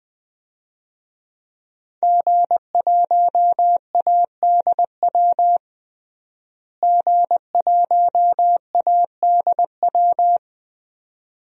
Morse code, heard as a keyed tone, about 20 wpm, 700 Hz. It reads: G1ADW G1ADW
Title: DXCC-100 Call Signs - 5 Characters - Repeated 2x - 20wpm